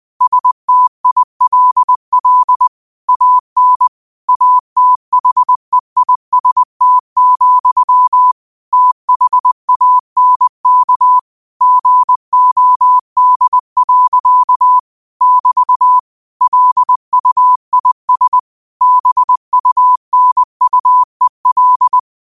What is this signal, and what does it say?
Morse code, heard as a keyed tone, about 20 wpm, 1000 Hz. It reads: STILL AN ATHEIST, THANK GOD. <BT> LUIS BUNUEL